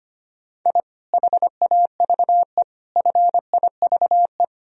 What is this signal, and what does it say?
Morse code, heard as a keyed tone, about 25 wpm, 700 Hz. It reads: I HAVE FIVE
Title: Sentences from Top 1000 Words - Review Part 2 - 25wpm